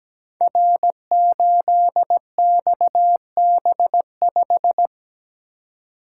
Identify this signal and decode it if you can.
Morse code, heard as a keyed tone, about 17 wpm, 700 Hz. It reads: R8XB5
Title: Code Groups - 5 Letter:Numbers - 17wpm